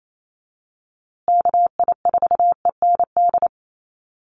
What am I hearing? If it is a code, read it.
Morse code, heard as a keyed tone, about 28 wpm, 700 Hz. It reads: KI4END
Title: USA Call Signs Repeated 3x 28wpm